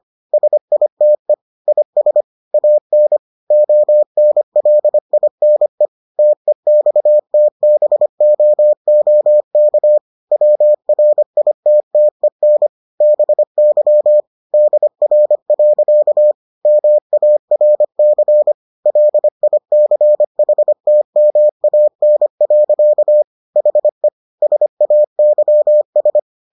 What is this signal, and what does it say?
Morse code, heard as a keyed tone, about 25 wpm, 600 Hz. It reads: SITE IS AN ONLINE TEXTBOOK WRITTEN BY DR. MARC LICHTMAN. HE SAYS